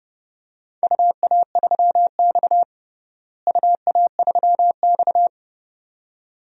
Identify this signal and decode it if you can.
Morse code, heard as a keyed tone, about 31 wpm, 700 Hz. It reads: UA3X UA3X
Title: DXCC-100 Call Signs - 4 Characters - Repeated 2x - 30wpm